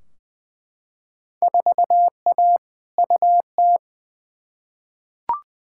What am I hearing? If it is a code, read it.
Morse code, heard as a keyed tone, about 21 wpm, 700 Hz. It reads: 4A UT
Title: Field Day Exchanges - 20wpm